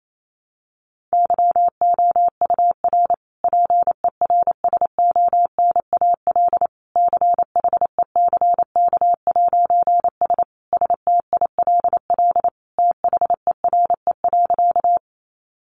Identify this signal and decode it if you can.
Morse code, heard as a keyed tone, about 28 wpm, 700 Hz. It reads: YOUR PERSONAL CHECK'S STILL THERE.
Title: A Sound Of Thunder 28wpm